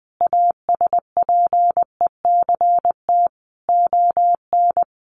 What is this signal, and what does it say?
Morse code, heard as a keyed tone, about 20 wpm, 700 Hz. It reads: ASPECT ON
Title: Sentences from Top 1300 Words 20wpm